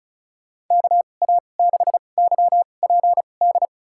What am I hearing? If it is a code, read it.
Morse code, heard as a keyed tone, about 36 wpm, spaced as for 25 wpm, 700 Hz. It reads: KA6YPD
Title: USA Call Signs - 6 Characters - 2x Character Spacing - 35wpm